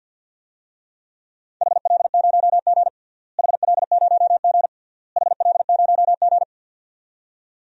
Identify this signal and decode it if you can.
Morse code, heard as a keyed tone, about 52 wpm, 700 Hz. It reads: HL0G HL0G HL0G
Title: DXCC-100 Call Signs - 4 Characters - Repeated 3x - 50wpm